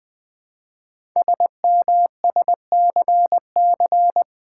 Morse code, heard as a keyed tone, about 20 wpm, 700 Hz. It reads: SMSCC
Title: Code Groups - 5 Letters - 20wpm